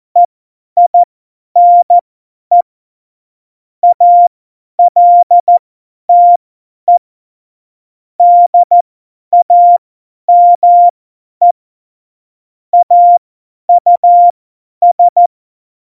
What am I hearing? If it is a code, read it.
Morse code, heard as a keyed tone, about 14 wpm, spaced as for 10 wpm, 700 Hz. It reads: EINE ALTE DAME AUS